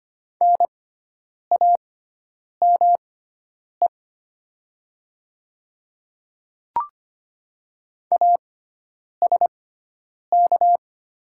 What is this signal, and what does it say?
Morse code, heard as a keyed tone, about 25 wpm, spaced as for 9 wpm, 700 Hz. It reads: NAME ASK
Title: 6x Character Spacing - Top 300 Words - 25wpm